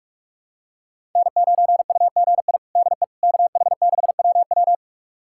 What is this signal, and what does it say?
Morse code, heard as a keyed tone, about 46 wpm, 700 Hz. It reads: N9UGI DE KH6WW
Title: USA Callsign DE Callsign - 5 Characters - 45wpm